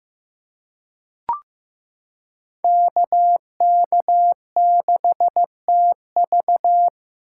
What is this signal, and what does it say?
Morse code, heard as a keyed tone, about 15 wpm, 700 Hz. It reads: KK6TV
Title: K1USN SST QSO Practice - 15wpm